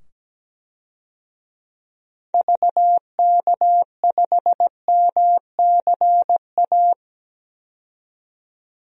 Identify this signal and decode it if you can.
Morse code, heard as a keyed tone, about 17 wpm, 700 Hz. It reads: VK5MCA
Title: DXCC-100 Call Signs - 6 Characters - 17wpm